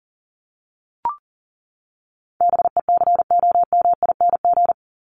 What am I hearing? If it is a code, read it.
Morse code, heard as a keyed tone, about 40 wpm, 700 Hz. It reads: BECOMING
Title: ING Suffix - Top 100 Words with Base Words 40wpm